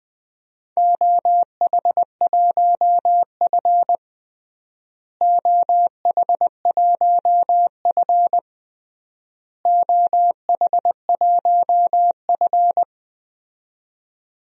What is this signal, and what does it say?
Morse code, heard as a keyed tone, about 20 wpm, 700 Hz. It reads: OH1F OH1F OH1F
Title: DXCC-100 Call Signs - 4 Characters - Repeated 3x - 20wpm